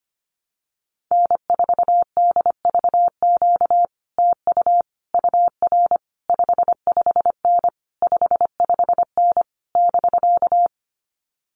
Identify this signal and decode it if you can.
Morse code, heard as a keyed tone, about 25 wpm, 700 Hz. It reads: N4DVQ TU UR 55N 55N <BK>
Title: Common QSO Phrases Encore 25wpm